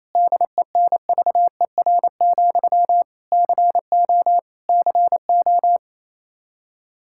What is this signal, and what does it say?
Morse code, heard as a keyed tone, about 28 wpm, 700 Hz. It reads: DENVER, CO CO
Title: Practice QSO - Exchange 1 - Beginning - 28wpm